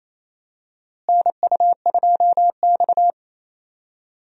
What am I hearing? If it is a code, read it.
Morse code, heard as a keyed tone, about 28 wpm, 700 Hz. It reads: NU2X
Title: USA Call Signs - 4 Characters - Repeated 3x - 28wpm